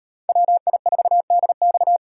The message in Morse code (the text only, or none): WI4DX